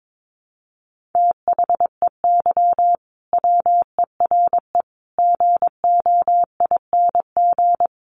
Morse code, heard as a keyed tone, about 22 wpm, 700 Hz. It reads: THEY WERE GOING
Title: Sets of 4 Words 22wpm